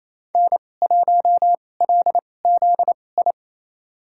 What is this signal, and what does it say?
Morse code, heard as a keyed tone, about 27 wpm, spaced as for 20 wpm, 700 Hz. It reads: N1LZI
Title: USA Call Signs - 2x Character Spacing - 28wpm